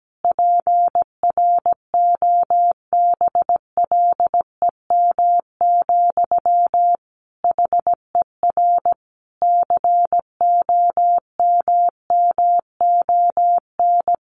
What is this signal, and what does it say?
Morse code, heard as a keyed tone, about 17 wpm, 700 Hz. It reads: PROBLEM, HER COMMON
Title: Sentences from Top 1200 Words 17wpm